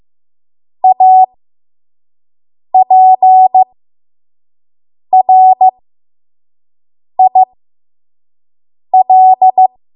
Morse code, heard as a keyed tone, about 15 wpm, spaced as for 5 wpm, 750 Hz. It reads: APRIL
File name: Morse / 05WPM